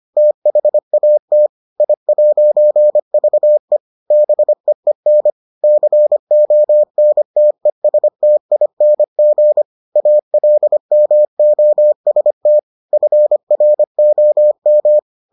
Morse code, heard as a keyed tone, about 25 wpm, 600 Hz. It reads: THAT I'VE BEEN CONTESTING ALMOST FROM